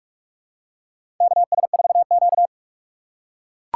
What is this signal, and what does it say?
Morse code, heard as a keyed tone, about 43 wpm, 700 Hz. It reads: KS4Q
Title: USA Call Signs - 4x Character Spacing - 45wpm